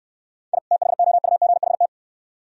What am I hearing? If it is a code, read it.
Morse code, heard as a keyed tone, about 68 wpm, 690 Hz. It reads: I THOUGHT